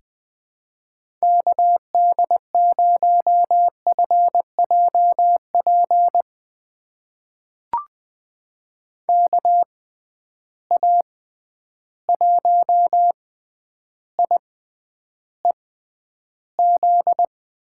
Morse code, heard as a keyed tone, about 20 wpm, spaced as for 10 wpm, 700 Hz. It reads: KD0FJP KA1IEZ
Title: USA Call Signs - 6 Characters - 6x Character Spacing - 20wpm